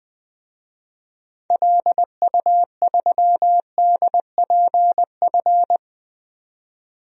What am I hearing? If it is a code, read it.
Morse code, heard as a keyed tone, about 20 wpm, 700 Hz. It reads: LU3DPF
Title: DXCC-100 Call Signs - 6 Characters - 20wpm